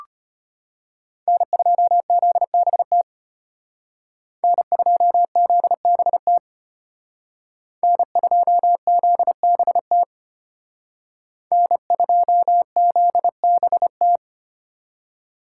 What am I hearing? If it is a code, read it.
Morse code, heard as a keyed tone, about 32 wpm, 700 Hz. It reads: N2ZBT N2ZBT N2ZBT N2ZBT